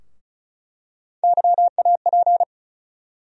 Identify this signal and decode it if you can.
Morse code, heard as a keyed tone, about 34 wpm, 700 Hz. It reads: YAP
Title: USA Call Sign Postfixes - 3 Letters 35wpm